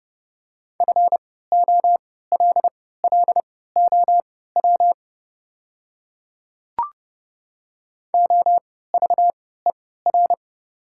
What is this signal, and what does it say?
Morse code, heard as a keyed tone, about 30 wpm, spaced as for 17 wpm, 700 Hz. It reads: FOLLOW OVER
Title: 3x Character Spacing - Top 300 Words - 30wpm